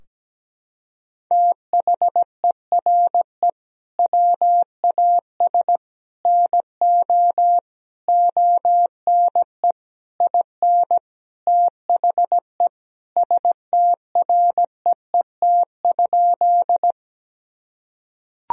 Morse code, heard as a keyed tone, about 17 wpm, 700 Hz. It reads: THERE WAS NO ONE IN THE STREET?
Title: Sentences from Top 1000 Words - Review Part 3 - 17wpm